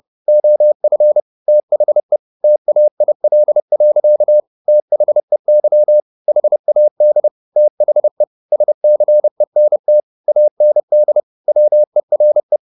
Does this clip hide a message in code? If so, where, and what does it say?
Yes: Morse code from 0.3 to 12.6 s, heard as a keyed tone, about 30 wpm, 600 Hz. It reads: OF THE TAIL. THEY HAD THE SCENT AND WERE